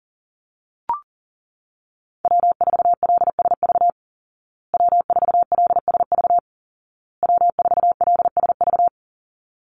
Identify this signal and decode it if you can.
Morse code, heard as a keyed tone, about 40 wpm, 700 Hz. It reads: W4LSV W4LSV W4LSV
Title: Call Signs Repeated 2x - 40wpm